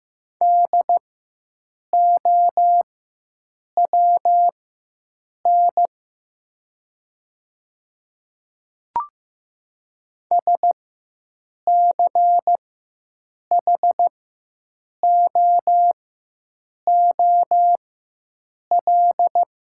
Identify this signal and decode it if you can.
Morse code, heard as a keyed tone, about 15 wpm, spaced as for 7 wpm, 700 Hz. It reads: DOWN SCHOOL
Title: 4x Character Spacing - Top 300 Words - 15wpm